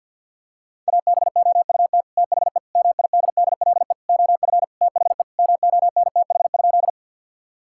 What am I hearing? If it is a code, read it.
Morse code, heard as a keyed tone, about 50 wpm, 700 Hz. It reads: ABOUT THE MIDDLE OF THE MONTH?
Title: Sentences from Top 1000 Words - Review Part 6 - 50wpm